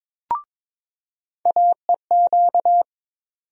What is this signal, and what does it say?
Morse code, heard as a keyed tone, about 22 wpm, 700 Hz. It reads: AEQ
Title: USA Call Sign Postfixes - 3 Letters 22wpm